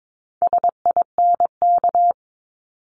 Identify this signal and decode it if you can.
Morse code, heard as a keyed tone, about 22 wpm, 700 Hz. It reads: SINK
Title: Multi-letter Phonemes 22wpm